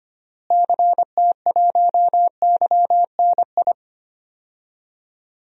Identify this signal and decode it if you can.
Morse code, heard as a keyed tone, about 25 wpm, 700 Hz. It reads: CT1YNI
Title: DXCC-100 Call Signs - 6 Characters - Repeated 2x - 25wpm